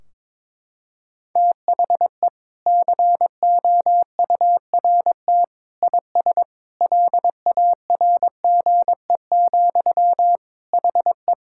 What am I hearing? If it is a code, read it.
Morse code, heard as a keyed tone, about 22 wpm, 700 Hz. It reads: THE COURT IS LARGE, HE
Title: Sentences from Top 1600 Words 22wpm